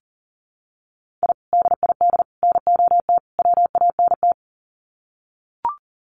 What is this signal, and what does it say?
Morse code, heard as a keyed tone, about 40 wpm, 700 Hz. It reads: I DID NOT WANT
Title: Sets of 4 Words 40wpm